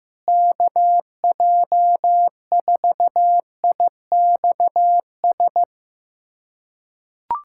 Morse code, heard as a keyed tone, about 16 wpm, 700 Hz. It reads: KJ4IXS
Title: USA Callsign DE Callsign 15wpm